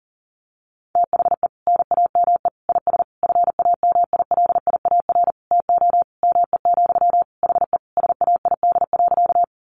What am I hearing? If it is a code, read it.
Morse code, heard as a keyed tone, about 40 wpm, 700 Hz. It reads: THE NAME IS FAMILIAR TO ME, HE SAID.
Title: Sentences from Top 1000 Words 40wpm